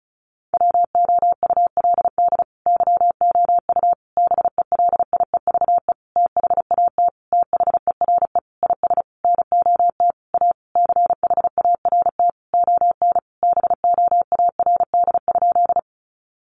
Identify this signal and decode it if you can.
Morse code, heard as a keyed tone, about 34 wpm, 700 Hz. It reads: WOULD YOU BELIEVE THAT THERE IS NOT A CHART ON BOARD?
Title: Sentences from Top 1300 Words 35wpm